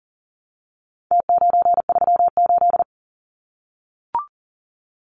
Morse code, heard as a keyed tone, about 40 wpm, 700 Hz. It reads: T938
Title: Numbers - Contest Style - 4-Digit 40wpm